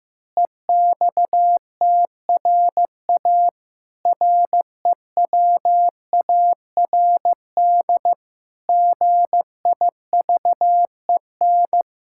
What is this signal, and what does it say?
Morse code, heard as a keyed tone, about 15 wpm, 700 Hz. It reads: EXTRA REWARD GIVEN
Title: Pangram Challenge v2 - 15wpm